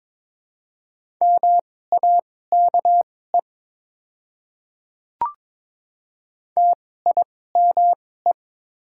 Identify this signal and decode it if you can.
Morse code, heard as a keyed tone, about 22 wpm, spaced as for 16 wpm, 700 Hz. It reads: MAKE TIME
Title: 2x Character Spacing - Top 200 Words - 22wpm